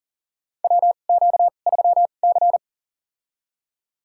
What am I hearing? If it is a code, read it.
Morse code, heard as a keyed tone, about 38 wpm, spaced as for 28 wpm, 700 Hz. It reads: WQ3C
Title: USA Call Signs - 4 Characters - 2x Character Spacing - 40wpm